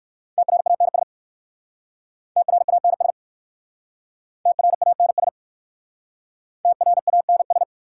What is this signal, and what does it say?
Morse code, heard as a keyed tone, about 59 wpm, 700 Hz. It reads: TRANS TRANS TRANS TRANS